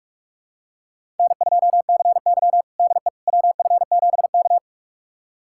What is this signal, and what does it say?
Morse code, heard as a keyed tone, about 45 wpm, 700 Hz. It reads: N2CY DE WF7K
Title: USA Callsign DE Callsign - 4 Characters - 45wpm